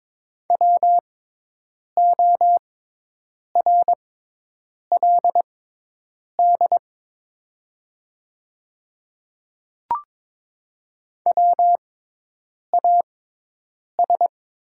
Morse code, heard as a keyed tone, about 22 wpm, spaced as for 8 wpm, 700 Hz. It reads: WORLD WAS
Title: 6x Character Spacing - Top 200 Words - 22wpm